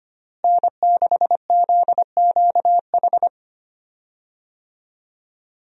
Morse code, heard as a keyed tone, about 25 wpm, 700 Hz. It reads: N6ZQH
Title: USA Call Signs - 5 Characters - Repeated 2x - 25wpm